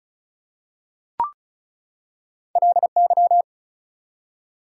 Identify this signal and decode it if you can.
Morse code, heard as a keyed tone, about 36 wpm, 700 Hz. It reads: LY
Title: Prefixes and Suffixes 35wpm